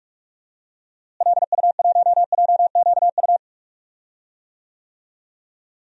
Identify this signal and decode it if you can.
Morse code, heard as a keyed tone, about 46 wpm, 700 Hz. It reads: LU1JQU
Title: DXCC-100 Call Signs - 6 Characters - Repeated 3x - 45wpm